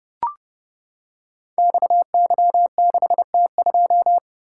Morse code, heard as a keyed tone, about 30 wpm, 700 Hz. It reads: XY6T2
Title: Code Groups - 5 Letter:Numbers - 30wpm